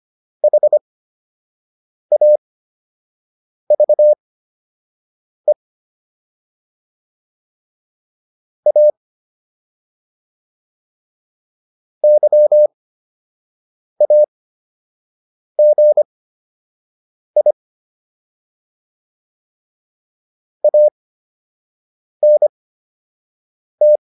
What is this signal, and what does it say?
Morse code, heard as a keyed tone, about 25 wpm, spaced as for 6 wpm, 600 Hz. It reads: HAVE A YAGI ANT